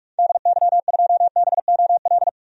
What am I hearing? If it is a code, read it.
Morse code, heard as a keyed tone, about 43 wpm, 700 Hz. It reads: DY2BOL
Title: DXCC-100 Call Signs - 6 Characters - 45wpm